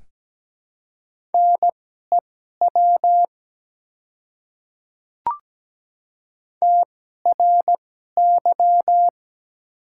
Morse code, heard as a keyed tone, about 17 wpm, spaced as for 12 wpm, 700 Hz. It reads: NEW TRY
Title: 2x Character Spacing - Top 200 Words - 17wpm